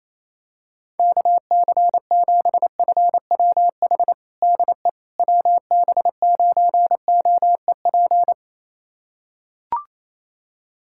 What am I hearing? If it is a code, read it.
Morse code, heard as a keyed tone, about 28 wpm, 700 Hz. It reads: KC7FWH DE WB9OEP
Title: USA Callsign DE Callsign - 6 Characters - 28wpm